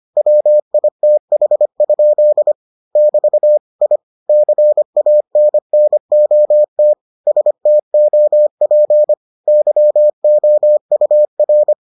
Morse code, heard as a keyed tone, about 25 wpm, 600 Hz. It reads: WITH? <BT> I CANNOT STOP YOUR